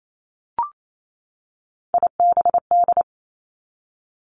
Morse code, heard as a keyed tone, about 28 wpm, 700 Hz. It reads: IBD
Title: Three Challenging Letter-Number Combinations 28wpm